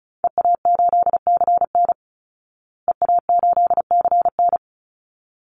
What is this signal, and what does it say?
Morse code, heard as a keyed tone, about 34 wpm, 700 Hz. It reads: EA8CN EA8CN
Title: Call Signs Repeated 3x - 35wpm